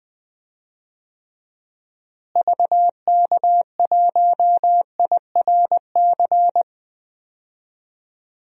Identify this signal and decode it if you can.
Morse code, heard as a keyed tone, about 21 wpm, 700 Hz. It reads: VK1IRC
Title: DXCC-100 Call Signs - 6 Characters - Repeated 2x - 20wpm